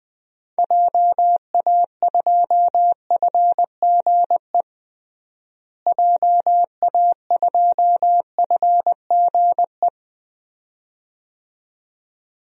Morse code, heard as a keyed tone, about 20 wpm, 700 Hz. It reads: JA2FGE JA2FGE
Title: DXCC-100 Call Signs - 6 Characters - Repeated 2x - 20wpm